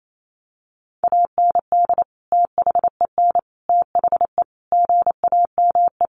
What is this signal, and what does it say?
Morse code, heard as a keyed tone, about 28 wpm, 700 Hz. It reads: AND THEN THE GAME
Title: Sentences from Top 500 Words 28wpm